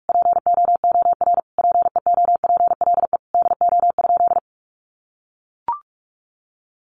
Morse code, heard as a keyed tone, about 44 wpm, 700 Hz. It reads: POOR PEOPLE DO?
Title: Sentences from Top 1100 Words 45wpm